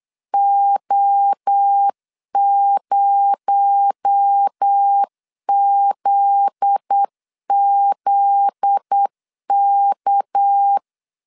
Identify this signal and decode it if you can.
Morse code, heard as a keyed tone, about 8 wpm, 800 Hz. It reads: O0ZZK
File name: CW / DET / 1 9 1500C S A 8wpm 5637